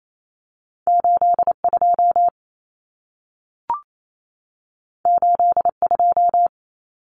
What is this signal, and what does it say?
Morse code, heard as a keyed tone, about 28 wpm, 700 Hz. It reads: 82 82
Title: Common Cut Numbers - 2-Digit 28wpm